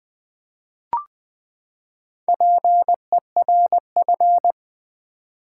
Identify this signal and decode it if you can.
Morse code, heard as a keyed tone, about 20 wpm, 700 Hz. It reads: PERF
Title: Build A Word - 4 Letters At A Time - 11-Letter Words - 20wpm